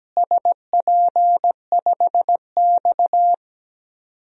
Morse code, heard as a keyed tone, about 17 wpm, 700 Hz. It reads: SP5X
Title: DXCC-100 Call Signs - 4 Characters - Repeated 2x - 17wpm